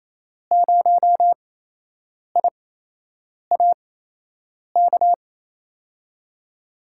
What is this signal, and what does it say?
Morse code, heard as a keyed tone, about 28 wpm, spaced as for 8 wpm, 700 Hz. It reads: 0IA K
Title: USA Call Signs - 6 Characters - 8x Character Spacing - 28wpm